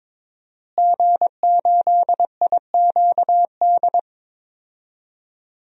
Morse code, heard as a keyed tone, about 22 wpm, 700 Hz. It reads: G8IQD
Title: DXCC-100 Call Signs - 5 Characters - Repeated 2x - 22wpm